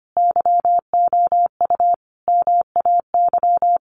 Morse code, heard as a keyed tone, about 25 wpm, 700 Hz. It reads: YOU MAY
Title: Sentences from Top 1500 Words 25wpm